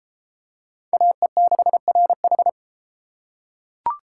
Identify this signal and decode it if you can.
Morse code, heard as a keyed tone, about 32 wpm, 700 Hz. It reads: AE6RH